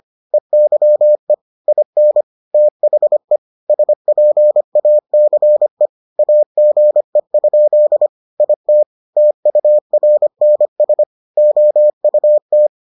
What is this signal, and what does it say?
Morse code, heard as a keyed tone, about 25 wpm, 600 Hz. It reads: EYE IN THE SPACE AGE? IT TURNS OUT